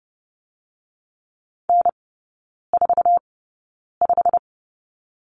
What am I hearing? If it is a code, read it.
Morse code, heard as a keyed tone, about 30 wpm, spaced as for 9 wpm, 700 Hz. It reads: N45
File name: course / 30:10wpm / 087 Lesson 15e - Number 5 - Three Characters 30wpm